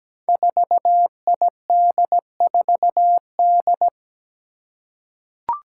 Morse code, heard as a keyed tone, about 17 wpm, 700 Hz. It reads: 4ID4D
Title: Five Challenging Letter-Number Combinations 17wpm